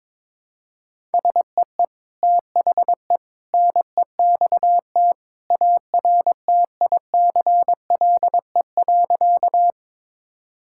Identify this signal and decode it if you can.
Morse code, heard as a keyed tone, about 22 wpm, 700 Hz. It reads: SEE THE NEXT ARTICLE.
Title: Sentences from Top 1000 Words - Review Part 5 - 22wpm